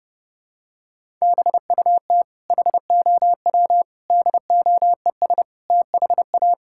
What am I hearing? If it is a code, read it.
Morse code, heard as a keyed tone, about 30 wpm, 700 Hz. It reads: BUT HOW DOES THA
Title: Sentences from Top 1000 Words - Review Part 6 - 30wpm